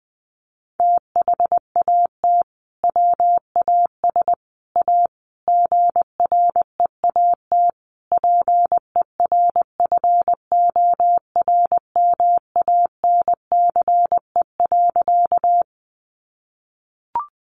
Morse code, heard as a keyed tone, about 20 wpm, 700 Hz. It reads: THAT WAS A GREAT PERFORMANCE.
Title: Sentences from Top 1000 Words 20wpm